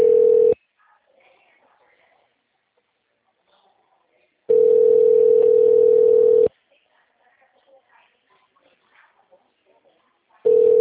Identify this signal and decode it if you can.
Ringback tone (North America), 440 + 480 Hz.